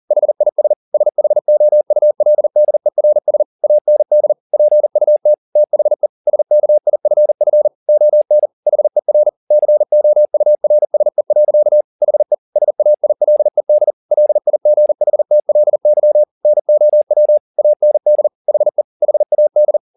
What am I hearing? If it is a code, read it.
Morse code, heard as a keyed tone, about 40 wpm, 600 Hz. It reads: HIS SHOULDERS AND PUT THE SKIFF ON HER COURSE. HE SAILED LIGHTLY NOW AND HE HAD
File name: the old man and the sea 2 40wpm